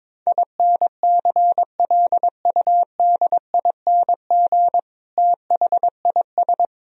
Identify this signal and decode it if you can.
Morse code, heard as a keyed tone, about 22 wpm, 700 Hz. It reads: INCLUDING THIS
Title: Sentences from Top 1000 Words - Review Part 4 - 22wpm